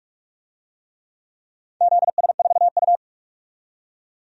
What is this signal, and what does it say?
Morse code, heard as a keyed tone, about 46 wpm, 700 Hz. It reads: ZS4U
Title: DXCC-100 Call Signs - 4 Characters - 45wpm